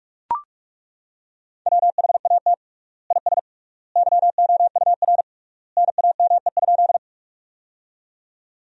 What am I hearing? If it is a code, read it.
Morse code, heard as a keyed tone, about 42 wpm, 700 Hz. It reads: WHAT IS YOUR NAME?
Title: 3x Word-Spacing - Sentences from Top 200 Words - 45wpm